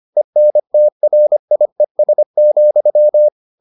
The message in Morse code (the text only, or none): ENTRIES,